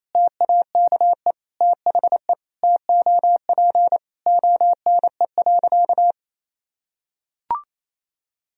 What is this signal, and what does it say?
Morse code, heard as a keyed tone, about 28 wpm, 700 Hz. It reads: TAKE THE TOP ONE.
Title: Sentences from Top 1000 Words - Review Part 2 - 28wpm